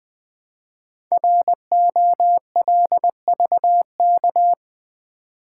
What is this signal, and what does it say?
Morse code, heard as a keyed tone, about 21 wpm, 700 Hz. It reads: ROLVK
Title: Code Groups - 5 Letter:Numbers - 20wpm